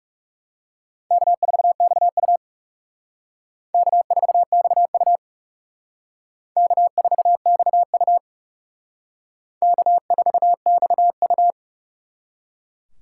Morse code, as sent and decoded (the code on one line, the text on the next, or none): -.- ....- -..- ..- / -.- ....- -..- ..- / -.- ....- -..- ..- / -.- ....- -..- ..-
K4XU K4XU K4XU K4XU